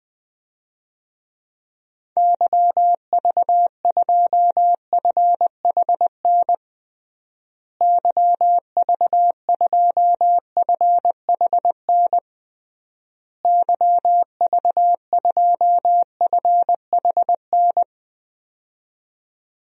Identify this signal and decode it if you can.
Morse code, heard as a keyed tone, about 20 wpm, 700 Hz. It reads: YV2FHN YV2FHN YV2FHN
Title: DXCC-100 Call Signs - 6 Characters - Repeated 3x - 20wpm